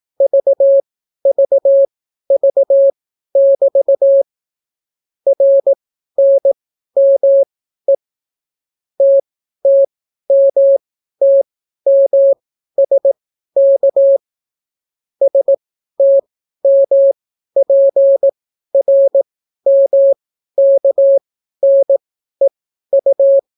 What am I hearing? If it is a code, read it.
Morse code, heard as a keyed tone, about 18 wpm, spaced as for 12 wpm, 550 Hz. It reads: VVV<BT> RNME TTMTMSK STMPRMKNEU